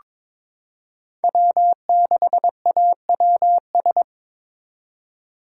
Morse code, heard as a keyed tone, about 22 wpm, 700 Hz. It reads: W6AWS
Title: K1USN SST QSO Practice - 22wpm